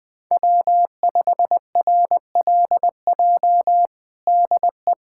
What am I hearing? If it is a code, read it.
Morse code, heard as a keyed tone, about 20 wpm, 700 Hz. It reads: W5RLJ DE
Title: USA Callsign DE Callsign - 5 Characters - 20wpm